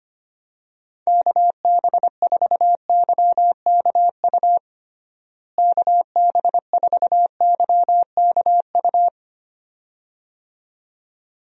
Morse code, heard as a keyed tone, about 25 wpm, 700 Hz. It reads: KB4YKU KB4YKU
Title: USA Call Signs - 6 Characters - Repeated 2x - 25wpm